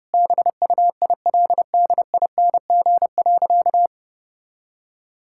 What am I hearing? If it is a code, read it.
Morse code, heard as a keyed tone, about 32 wpm, 700 Hz. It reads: BUILDING.
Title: Sentences from Top 1000 Words - Review Part 5 - 30wpm